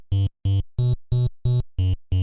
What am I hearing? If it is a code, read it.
There is no code.